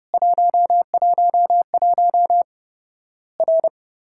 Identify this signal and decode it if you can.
Morse code, heard as a keyed tone, about 32 wpm, 700 Hz. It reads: 111 R
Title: Worked All Europe DX Contest - QTC Exchange - 30wpm